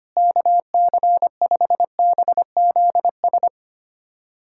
Morse code, heard as a keyed tone, about 25 wpm, 700 Hz. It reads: KC5BZS
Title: USA Callsign DE Callsign 25wpm